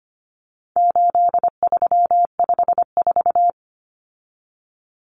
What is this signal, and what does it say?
Morse code, heard as a keyed tone, about 25 wpm, 700 Hz. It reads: 8354
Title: Numbers - 4-Digit 25wpm